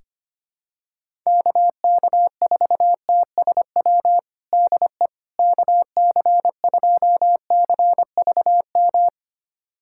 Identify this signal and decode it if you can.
Morse code, heard as a keyed tone, about 25 wpm, 700 Hz. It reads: KK4TSW DE KC2CVM
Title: USA Callsign DE Callsign - 6 Characters - 25wpm